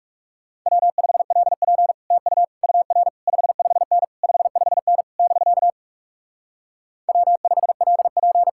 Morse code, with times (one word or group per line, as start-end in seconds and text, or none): 0.7-1.9 W5LP
2.1-2.4 TU
2.6-3.1 UR
3.3-4.0 55N
4.2-5.0 55N
5.2-5.7 <BK>
7.1-8.5 W5LP